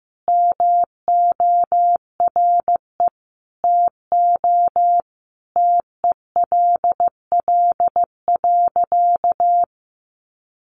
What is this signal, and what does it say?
Morse code, heard as a keyed tone, about 15 wpm, 700 Hz. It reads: MORE TO TELL.
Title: Sentences from Top 200 Words 15wpm